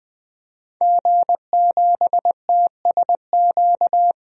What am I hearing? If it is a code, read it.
Morse code, heard as a keyed tone, about 20 wpm, 700 Hz. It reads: G7TSQ
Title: DXCC-100 Call Signs - 5 Characters - 20wpm